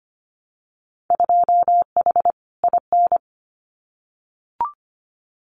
Morse code, heard as a keyed tone, about 25 wpm, 700 Hz. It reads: 2H IN
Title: Winter Field Day Exchanges Encore 25wpm